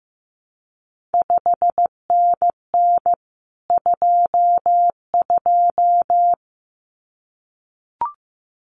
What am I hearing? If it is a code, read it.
Morse code, heard as a keyed tone, about 15 wpm, 700 Hz. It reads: 5NN 22
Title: CQ WW DX Contest Exchange Encore - 15wpm